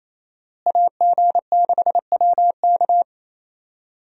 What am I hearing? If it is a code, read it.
Morse code, heard as a keyed tone, about 27 wpm, 700 Hz. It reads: AG6WK
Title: USA Call Signs - 5 Characters - Repeated 3x - 28wpm